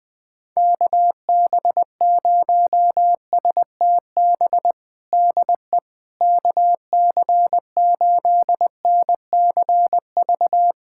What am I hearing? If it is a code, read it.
Morse code, heard as a keyed tone, about 20 wpm, 700 Hz. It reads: KB0STB DE KC8NCV
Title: USA Callsign DE Callsign - 6 Characters - 20wpm